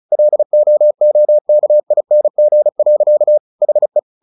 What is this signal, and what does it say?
Morse code, heard as a keyed tone, about 34 wpm, 600 Hz. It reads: LOOKING. HE
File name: the old man and the sea / the old man and the sea 1 35wpm